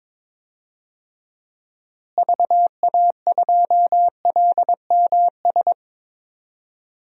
Morse code, heard as a keyed tone, about 22 wpm, 700 Hz. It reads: VA2LMS
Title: DXCC-100 Call Signs - 6 Characters - Repeated 2x - 22wpm